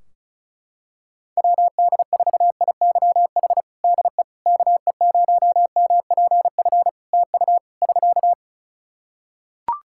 Morse code, heard as a keyed tone, about 34 wpm, 700 Hz. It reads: WD4IYH DE KE0MPF TU <SK>
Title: Practice QSO - Exchange 4 - Ending - 35wpm